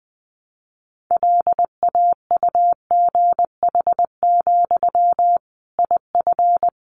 Morse code, heard as a keyed tone, about 20 wpm, 700 Hz. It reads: LAUGH, IF
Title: Sentences from Top 500 Words 20wpm